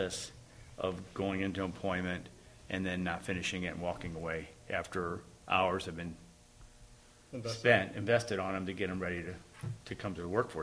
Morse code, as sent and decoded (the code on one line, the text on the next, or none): none